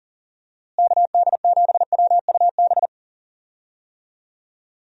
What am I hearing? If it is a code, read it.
Morse code, heard as a keyed tone, about 37 wpm, 700 Hz. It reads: KD7WUB
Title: USA Call Signs - 6 Characters - Repeated 3x - 40wpm